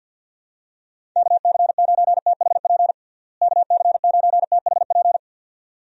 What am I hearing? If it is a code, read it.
Morse code, heard as a keyed tone, about 51 wpm, 700 Hz. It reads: KC9THP KC9THP
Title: USA Call Signs - 6 Characters - Repeated 2x - 50wpm